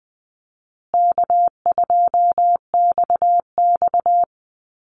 Morse code, heard as a keyed tone, about 20 wpm, 700 Hz. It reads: K2XX
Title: Call Signs Repeated 3x - 20wpm